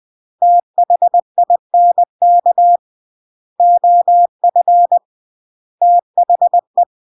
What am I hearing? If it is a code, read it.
Morse code, heard as a keyed tone, about 20 wpm, 700 Hz. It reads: THINK OF THE